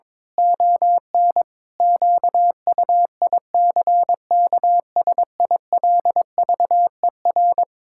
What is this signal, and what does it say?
Morse code, heard as a keyed tone, about 22 wpm, 700 Hz. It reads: ON QUICKSILVER
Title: Pangram Challenge v2 - 22wpm